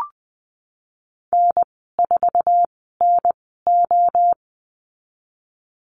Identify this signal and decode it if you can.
Morse code, heard as a keyed tone, about 20 wpm, 700 Hz. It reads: N 4 N O
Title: 062 Lesson 11g - Letter h - Call Signs 20wpm